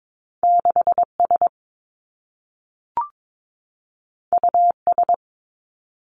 Morse code, heard as a keyed tone, about 22 wpm, 700 Hz. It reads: 6 S U S
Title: Two Challenging Letter-Number Combinations 22wpm